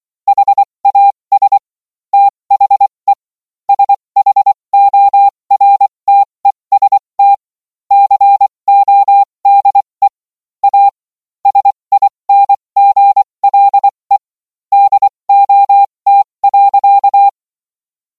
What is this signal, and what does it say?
Morse code, heard as a keyed tone, about 24 wpm, spaced as for 20 wpm, 800 Hz. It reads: HAS THE SHORTEST CODE A SINGLE DOT.